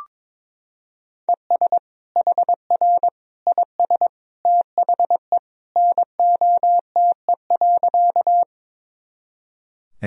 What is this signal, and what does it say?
Morse code, heard as a keyed tone, about 22 wpm, 700 Hz. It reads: ES HR IS THE NOTE.
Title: Sentences from Top 500 Words and Abbreviations 22wpm